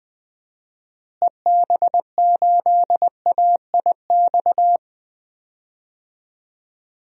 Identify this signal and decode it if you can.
Morse code, heard as a keyed tone, about 20 wpm, 700 Hz. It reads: EB8AIX
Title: DXCC-100 Call Signs - 6 Characters - Repeated 3x - 20wpm